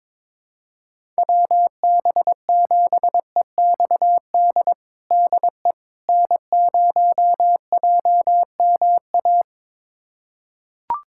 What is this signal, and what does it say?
Morse code, heard as a keyed tone, about 21 wpm, 700 Hz. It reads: WB7EXD DE N0JMA
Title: USA Callsign DE Callsign 22wpm